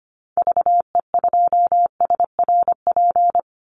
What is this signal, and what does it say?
Morse code, heard as a keyed tone, about 25 wpm, 700 Hz. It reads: VE2SRP
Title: Call Signs - 25wpm